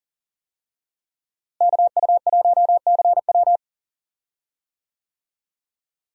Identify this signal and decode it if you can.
Morse code, heard as a keyed tone, about 37 wpm, 700 Hz. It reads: KU1CW
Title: USA Call Signs - 5 Characters - Repeated 3x - 40wpm